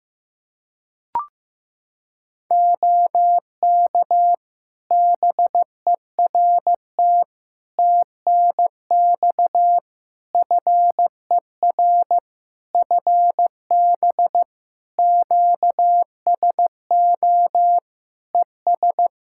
Morse code, heard as a keyed tone, about 15 wpm, 700 Hz. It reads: OK BERT TNX FER FB QSO ES
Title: Practice QSO - Exchange 4 - Ending - 15wpm